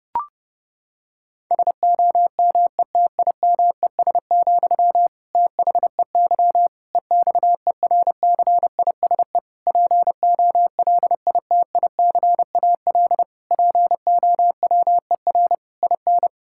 Morse code, heard as a keyed tone, about 30 wpm, 700 Hz. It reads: SOMETIMES, THEY EXERCISE POLITICAL POWER IN
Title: Sentences from Top 1000 Words - Review Part 6 - 30wpm